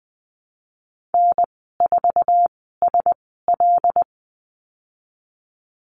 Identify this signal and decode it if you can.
Morse code, heard as a keyed tone, about 20 wpm, spaced as for 15 wpm, 700 Hz. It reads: N4SL
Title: 122 Lesson 20g - Number 3 - Call Signs 20wpm